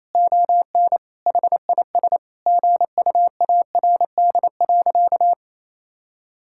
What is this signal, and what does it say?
Morse code, heard as a keyed tone, about 28 wpm, 700 Hz. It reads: ON HIS GUARD.
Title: Sentences from Top 1000 Words - Review Part 5 - 28wpm